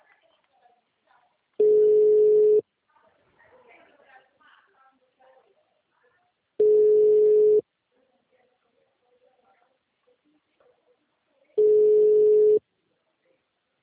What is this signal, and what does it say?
Ringback tone (Europe, 425 Hz), with speech.
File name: out- -2009- - - .9